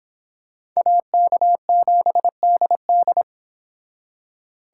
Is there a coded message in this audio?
Yes: Morse code from 0.8 to 3.2 s, heard as a keyed tone, about 26 wpm, 700 Hz. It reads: AK7DD